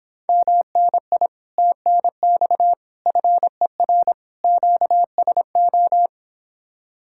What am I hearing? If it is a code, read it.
Morse code, heard as a keyed tone, about 26 wpm, 700 Hz. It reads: MNI TNX FER QSO